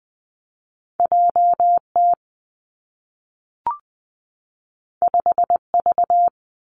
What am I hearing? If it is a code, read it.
Morse code, heard as a keyed tone, about 20 wpm, 700 Hz. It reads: J T 5 V
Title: Two Challenging Letter-Number Combinations 20wpm